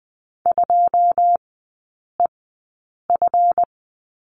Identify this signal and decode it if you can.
Morse code, heard as a keyed tone, about 20 wpm, spaced as for 8 wpm, 700 Hz. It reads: 2EF
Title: 145 Lesson 24e - Letter y - Three Characters 20wpm